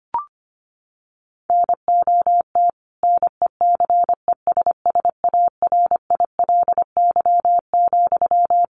Morse code, heard as a keyed tone, about 25 wpm, 700 Hz. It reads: NOT NECESSARILY,
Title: Sentences from Top 1400 Words 25wpm